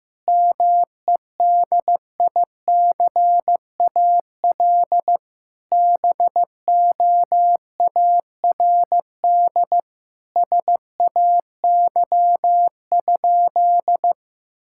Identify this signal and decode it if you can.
Morse code, heard as a keyed tone, about 15 wpm, 700 Hz. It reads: MEDICAL BOARD SAY?
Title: Sentences from Top 1000 Words - Review Part 5 - 15wpm